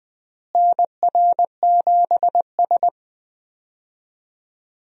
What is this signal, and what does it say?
Morse code, heard as a keyed tone, about 20 wpm, 700 Hz. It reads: NR7S
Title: USA Call Signs - 4 Characters - Repeated 3x - 20wpm